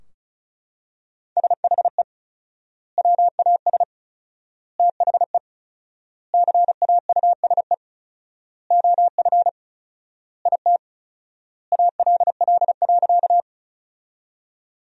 Morse code, heard as a keyed tone, about 34 wpm, 700 Hz. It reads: SHE WAS THE CAUSE OF IT ALL.
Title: 4x Word-Spacing - Sentences from Top 200 Words - 35wpm